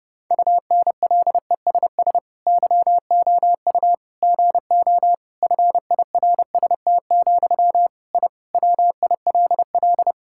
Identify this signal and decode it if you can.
Morse code, heard as a keyed tone, about 30 wpm, 700 Hz. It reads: UNLESS YOU GO FIRST, I WILL
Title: Sentences from Top 1000 Words - Review Part 6 - 30wpm